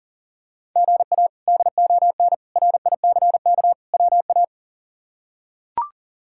Morse code, heard as a keyed tone, about 40 wpm, 700 Hz. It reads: GA DON RICK WA
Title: K1USN SST QSO Practice - 40wpm